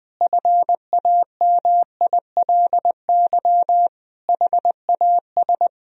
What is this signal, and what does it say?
Morse code, heard as a keyed tone, about 20 wpm, 700 Hz. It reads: FAMILY HAS
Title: Sentences from Top 1000 Words - Review Part 2 - 20wpm